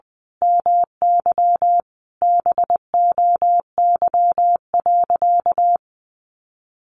Morse code, heard as a keyed tone, about 20 wpm, 700 Hz. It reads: MY BOY.
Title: Sentences from Top 800 Words 20wpm